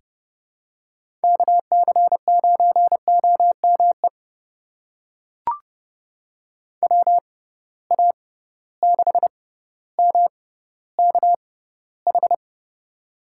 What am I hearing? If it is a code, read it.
Morse code, heard as a keyed tone, about 30 wpm, spaced as for 15 wpm, 700 Hz. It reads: KC9OME WA6MKH
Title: USA Call Signs - 6 Characters - 6x Character Spacing - 30wpm